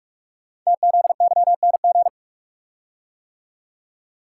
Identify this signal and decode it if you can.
Morse code, heard as a keyed tone, about 44 wpm, 700 Hz. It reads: TZYNG
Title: Code Groups - 5 Letter:Numbers - 45wpm